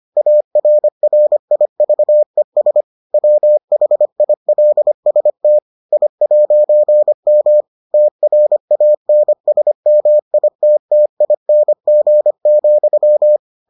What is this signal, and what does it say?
Morse code, heard as a keyed tone, about 25 wpm, 600 Hz. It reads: ARRIVES WHILST I'M TRANSMITTING,